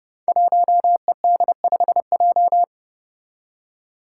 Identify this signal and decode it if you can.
Morse code, heard as a keyed tone, about 30 wpm, 700 Hz. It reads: 1ED5J
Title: Five Challenging Letter-Number Combinations 30wpm